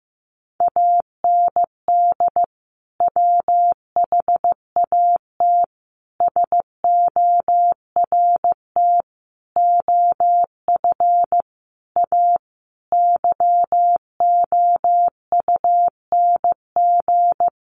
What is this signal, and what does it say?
Morse code, heard as a keyed tone, about 15 wpm, 700 Hz. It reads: AND WHAT SORT OF A YOUNG
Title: Sentences from Top 1300 Words 15wpm